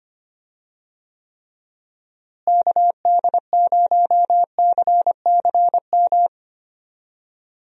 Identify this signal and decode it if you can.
Morse code, heard as a keyed tone, about 25 wpm, 700 Hz. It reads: KD0CCM
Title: USA Call Signs - 6 Characters - Repeated 2x - 25wpm